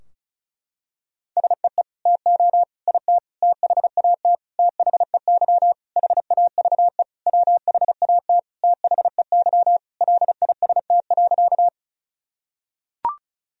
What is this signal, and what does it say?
Morse code, heard as a keyed tone, about 34 wpm, 700 Hz. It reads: SEE TO IT THAT THEY HAVE WHAT THEY LIST.
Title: Sentences from Top 1000 Words - Review Part 2 - 35wpm